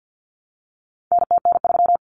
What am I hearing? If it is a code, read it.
Morse code, heard as a keyed tone, about 50 wpm, 700 Hz. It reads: NTN3
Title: Common Cut Numbers - 4-Digit 50wpm